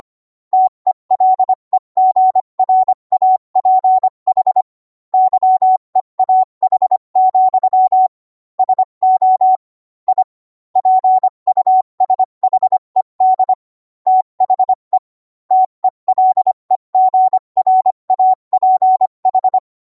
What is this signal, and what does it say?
Morse code, heard as a keyed tone, about 25 wpm, 750 Hz. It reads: TELEGRAPH YEAH, SO I PUSHED THE TELEGRAPH